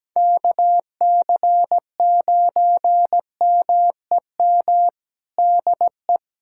Morse code, heard as a keyed tone, about 17 wpm, 700 Hz. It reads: KC9MEM DE
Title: USA Callsign DE Callsign - 6 Characters - 17wpm